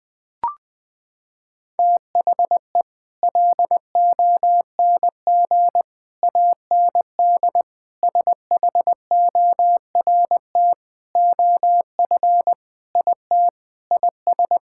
Morse code, heard as a keyed tone, about 20 wpm, 700 Hz. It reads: THE LONG AND SHORT OF IT IS
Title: Sets of 8 Words - 20wpm